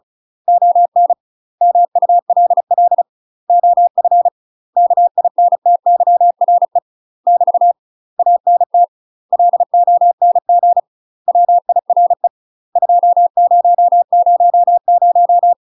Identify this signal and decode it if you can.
Morse code, heard as a keyed tone, about 36 wpm, 700 Hz. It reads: ON MULL OF KINTYRE <BT> ANT LONG WIRE 2000
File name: UK MASTER QSOs TRAINING / 35WPM / UK MASTER QSOs 0018